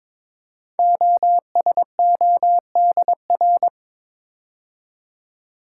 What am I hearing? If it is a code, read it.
Morse code, heard as a keyed tone, about 22 wpm, 700 Hz. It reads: OSODR
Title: Code Groups - 5 Letters - 22wpm